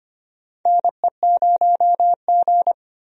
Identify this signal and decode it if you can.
Morse code, heard as a keyed tone, about 25 wpm, 700 Hz. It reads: NE0G
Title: USA Call Signs - 4 Characters - 25wpm